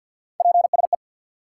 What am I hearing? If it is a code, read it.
Morse code, heard as a keyed tone, about 50 wpm, 700 Hz. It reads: PSE
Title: Top 90 QSO Elements 50wpm